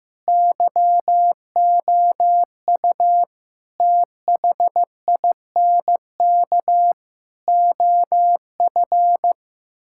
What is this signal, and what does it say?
Morse code, heard as a keyed tone, about 16 wpm, 700 Hz. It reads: YOU THINK OF
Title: Sentences from Top 1600 Words 15wpm